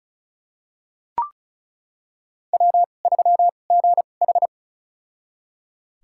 Morse code, heard as a keyed tone, about 34 wpm, spaced as for 25 wpm, 700 Hz. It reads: W3GH
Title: USA Call Signs - 2x Character Spacing - 35wpm